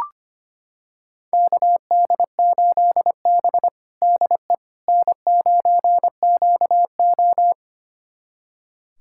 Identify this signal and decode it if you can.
Morse code, heard as a keyed tone, about 25 wpm, 700 Hz. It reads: KD8B DE N9QO
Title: USA Callsign DE Callsign - 4 Characters - 25wpm